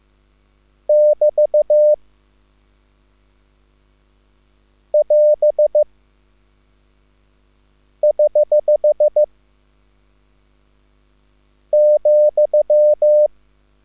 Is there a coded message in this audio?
Yes: Morse code from 0.9 to 13.3 s, heard as a keyed tone, about 15 wpm, 600 Hz. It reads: <BT> <AS> <HH> ,